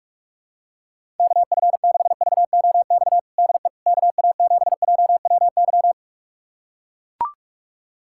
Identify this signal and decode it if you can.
Morse code, heard as a keyed tone, about 41 wpm, 700 Hz. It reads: KF6VOX DE KA7JWY
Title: USA Callsign DE Callsign - 6 Characters - 45wpm